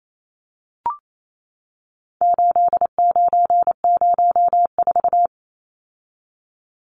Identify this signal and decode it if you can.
Morse code, heard as a keyed tone, about 28 wpm, 700 Hz. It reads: 8904
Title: Numbers - 4-Digit 28wpm